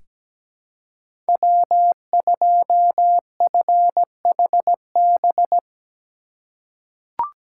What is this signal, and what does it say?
Morse code, heard as a keyed tone, about 17 wpm, 700 Hz. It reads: W2FHB
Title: USA Call Signs - 2x Character Spacing - 17wpm